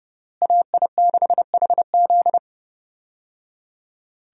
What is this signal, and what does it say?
Morse code, heard as a keyed tone, about 30 wpm, 700 Hz. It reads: AI6HZ
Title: USA Call Signs - 5 Characters - Repeated 3x - 30wpm